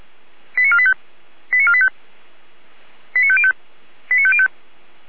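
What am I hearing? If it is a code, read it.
Five-tone selective call (ZVEI), heard as quick runs of five tones, tones of about 80 ms. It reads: 89485 89485 89505 89505
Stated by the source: ZVEI five-tone call received by SDR.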